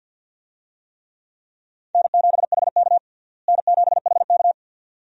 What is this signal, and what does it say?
Morse code, heard as a keyed tone, about 46 wpm, 700 Hz. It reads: N7HK N7HK
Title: USA Call Signs - 4 Characters - Repeated 2x - 50wpm